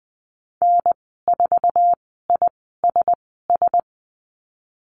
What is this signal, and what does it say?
Morse code, heard as a keyed tone, about 20 wpm, spaced as for 14 wpm, 700 Hz. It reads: N4ISS
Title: 062 Lesson 11g - Letter h - Call Signs 20wpm